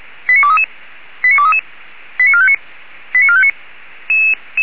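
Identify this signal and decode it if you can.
Five-tone selective call (ZVEI), heard as quick runs of five tones, tones of about 80 ms. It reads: 89233 89233 89459 89459
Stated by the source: ZVEI five-tone selective call received on an SDR.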